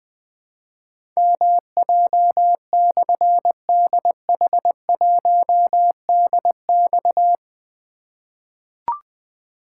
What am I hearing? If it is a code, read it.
Morse code, heard as a keyed tone, about 20 wpm, 700 Hz. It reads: MJ/DH1DX
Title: Difficult Call Signs 20wpm